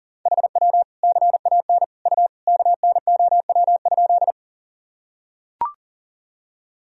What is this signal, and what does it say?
Morse code, heard as a keyed tone, about 40 wpm, 700 Hz. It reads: HW CAN U KNOW?
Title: Sentences from Top 100 Words and Abbreviations 40wpm